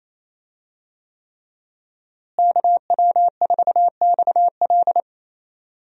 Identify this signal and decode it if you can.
Morse code, heard as a keyed tone, about 28 wpm, 700 Hz. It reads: KW4XL
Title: USA Call Signs - 5 Characters - Repeated 3x - 28wpm